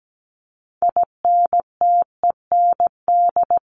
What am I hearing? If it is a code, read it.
Morse code, heard as a keyed tone, about 17 wpm, 700 Hz. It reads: INTEND
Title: Top 1501-1750 Words 17wpm